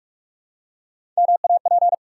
Morse code, heard as a keyed tone, about 44 wpm, 700 Hz. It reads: MAP
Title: Three Letter Words - 45wpm